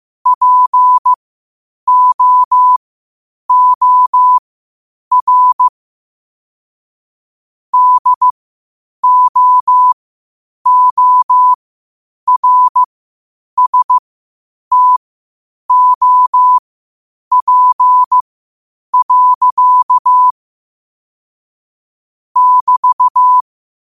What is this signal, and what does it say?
Morse code, heard as a keyed tone, about 15 wpm, spaced as for 8 wpm, 1000 Hz. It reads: POOR DOORSTOP. <BT>
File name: qotdfiles / 8wpm